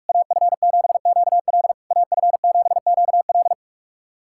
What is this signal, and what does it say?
Morse code, heard as a keyed tone, about 46 wpm, 690 Hz. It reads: AF7QL AF7QL
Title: SOTA QSO Practice 45wpm